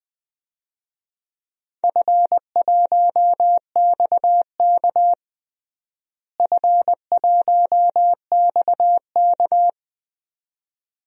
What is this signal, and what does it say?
Morse code, heard as a keyed tone, about 21 wpm, 700 Hz. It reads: F1XK F1XK
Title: DXCC-100 Call Signs - 4 Characters - Repeated 2x - 20wpm